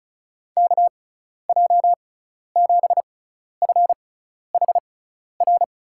Morse code, heard as a keyed tone, about 35 wpm, spaced as for 12 wpm, 690 Hz. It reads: KJ7FHR
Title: USA Call Signs - 6 Characters - 6x Character Spacing - 35wpm